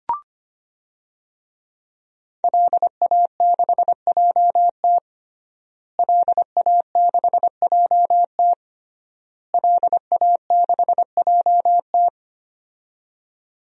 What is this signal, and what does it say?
Morse code, heard as a keyed tone, about 25 wpm, 700 Hz. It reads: LA6JT LA6JT LA6JT
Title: DXCC-100 Call Signs - 5 Characters - Repeated 3x - 25wpm